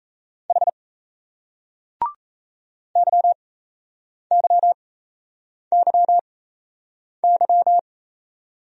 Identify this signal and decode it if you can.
Morse code, heard as a keyed tone, about 38 wpm, spaced as for 8 wpm, 700 Hz. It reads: H YYYY